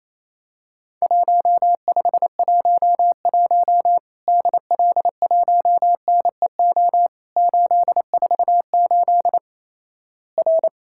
Morse code, heard as a keyed tone, about 28 wpm, 700 Hz. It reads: 1511 DL1NEO 848 R
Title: Worked All Europe DX Contest - QTC Exchange - 28wpm